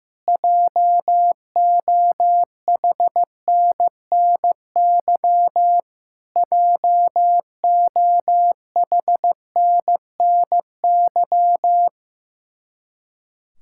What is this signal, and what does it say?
Morse code, heard as a keyed tone, about 15 wpm, 700 Hz. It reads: JOHNNY JOHNNY
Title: Practice QSO - Exchange 1 - Beginning - 15wpm